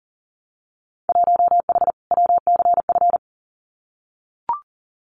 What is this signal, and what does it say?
Morse code, heard as a keyed tone, about 41 wpm, 700 Hz. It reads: 1H WCF
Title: Winter Field Day Exchanges 40wpm